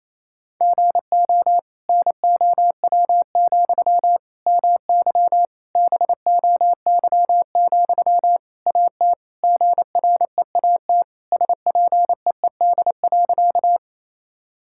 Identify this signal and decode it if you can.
Morse code, heard as a keyed tone, about 28 wpm, 700 Hz. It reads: GO NOW, MY BOY, AT GREAT SPEED.
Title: Sentences from Top 1000 Words - Review Part 5 - 28wpm